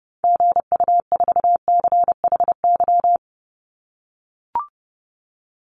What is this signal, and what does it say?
Morse code, heard as a keyed tone, about 30 wpm, 700 Hz. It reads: GU4CHY
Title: Call Signs - 30wpm